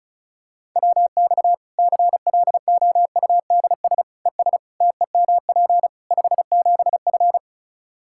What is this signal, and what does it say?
Morse code, heard as a keyed tone, about 36 wpm, 700 Hz. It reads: WX CLOUDS ES TEMP 57F
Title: Practice QSO - Exchange 2 - Info - 35wpm